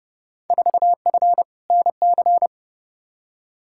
Morse code, heard as a keyed tone, about 30 wpm, 700 Hz. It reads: 4F NC
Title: Field Day Exchanges - 30wpm